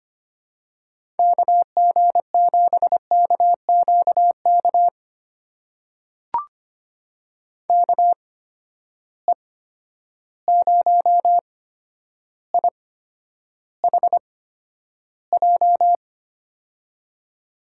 Morse code, heard as a keyed tone, about 25 wpm, spaced as for 11 wpm, 700 Hz. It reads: KG7KQK KE0IHJ
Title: USA Call Signs - 6 Characters - 8x Character Spacing - 25wpm